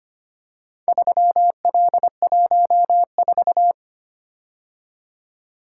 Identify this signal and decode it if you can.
Morse code, heard as a keyed tone, about 25 wpm, 700 Hz. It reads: 3L14
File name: practice / 25wpm / Four Letter-Number Combinations - 25wpm